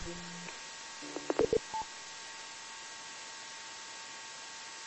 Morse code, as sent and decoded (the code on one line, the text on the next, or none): none